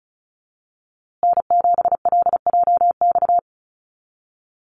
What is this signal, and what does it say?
Morse code, heard as a keyed tone, about 34 wpm, 700 Hz. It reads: N7LJX
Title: USA Call Signs Repeated 2x 35wpm